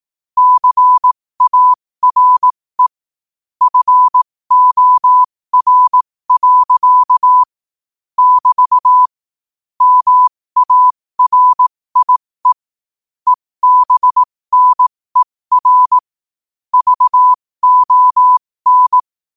Morse code, heard as a keyed tone, about 18 wpm, spaced as for 15 wpm, 1000 Hz. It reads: CARE FOR. <BT> MARIE EBNER VON
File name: qotdfiles / 15wpm